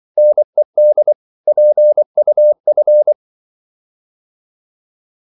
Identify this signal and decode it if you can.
Morse code, heard as a keyed tone, about 24 wpm, 600 Hz. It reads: NED PUF
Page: Korte danske ord 24wpm